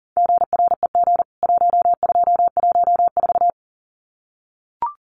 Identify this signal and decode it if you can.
Morse code, heard as a keyed tone, about 40 wpm, 700 Hz. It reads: GREG 1214
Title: CWT Exchange 40wpm